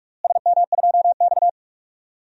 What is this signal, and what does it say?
Morse code, heard as a keyed tone, about 48 wpm, 700 Hz. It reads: SM2X
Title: DXCC-100 Call Signs - 4 Characters - Repeated 3x - 45wpm